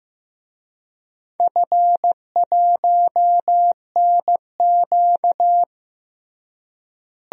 Morse code, heard as a keyed tone, about 15 wpm, 700 Hz. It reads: F1NQ
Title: DXCC-100 Call Signs - 4 Characters - Repeated 2x - 15wpm